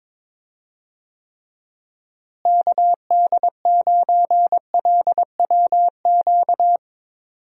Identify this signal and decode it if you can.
Morse code, heard as a keyed tone, about 22 wpm, 700 Hz. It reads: KD9LWQ
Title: USA Call Signs - 6 Characters - Repeated 3x - 22wpm